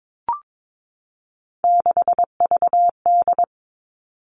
Morse code, heard as a keyed tone, about 22 wpm, 700 Hz. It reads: 6VD
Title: All Cut Numbers - 3-Digit 22wpm